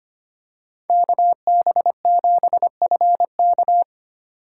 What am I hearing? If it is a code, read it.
Morse code, heard as a keyed tone, about 25 wpm, 700 Hz. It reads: KB7FK
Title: USA Call Signs - 5 Characters - 3x Character Spacing - 25wpm